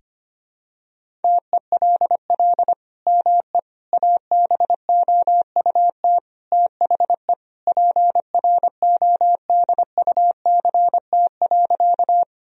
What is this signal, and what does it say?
Morse code, heard as a keyed tone, about 25 wpm, 700 Hz. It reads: TELL ME ABOUT THE PRODUCT.
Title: Sentences from Top 1000 Words - Review Part 2 - 25wpm